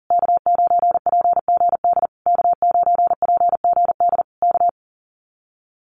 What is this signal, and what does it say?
Morse code, heard as a keyed tone, about 40 wpm, 700 Hz. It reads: K9PGD K9PGD K
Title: Common QSO Phrases Encore 40wpm